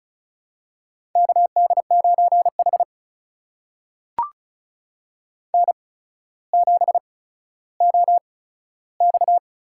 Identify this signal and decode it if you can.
Morse code, heard as a keyed tone, about 36 wpm, spaced as for 15 wpm, 700 Hz. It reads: KD9H N7OX
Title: USA Call Signs - 4 Characters - 8x Character Spacing - 35wpm